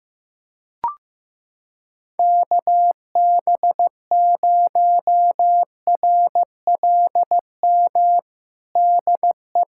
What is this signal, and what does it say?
Morse code, heard as a keyed tone, about 15 wpm, 700 Hz. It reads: KB0RLM DE
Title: USA Callsign DE Callsign - 6 Characters - 15wpm